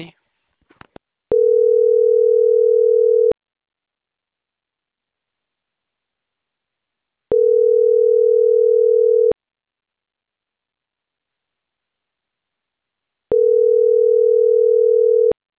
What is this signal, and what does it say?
Ringback tone (North America), 440 + 480 Hz.